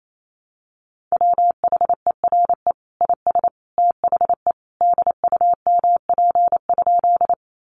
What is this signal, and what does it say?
Morse code, heard as a keyed tone, about 28 wpm, 700 Hz. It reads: WHERE IS THE DUMP?
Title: Sentences from Top 1300 Words 28wpm